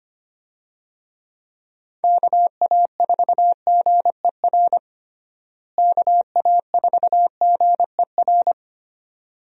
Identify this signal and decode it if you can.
Morse code, heard as a keyed tone, about 25 wpm, 700 Hz. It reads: KA4GER KA4GER
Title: USA Call Signs - 6 Characters - Repeated 2x - 25wpm